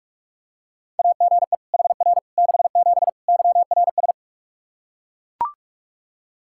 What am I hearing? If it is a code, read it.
Morse code, heard as a keyed tone, about 41 wpm, 700 Hz. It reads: AGE HR 67 YRS
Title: Practice QSO - Exchange 3 - More Info - 45wpm